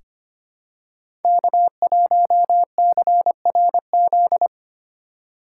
Morse code, heard as a keyed tone, about 25 wpm, 700 Hz. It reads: K1CRZ
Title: USA Call Signs - 5 Characters - 2x Character Spacing - 25wpm